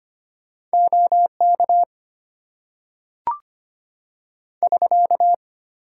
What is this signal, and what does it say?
Morse code, heard as a keyed tone, about 25 wpm, 700 Hz. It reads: OK <SK>
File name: Top 90 QSO Elements - 8x Character Spacing - 25wpm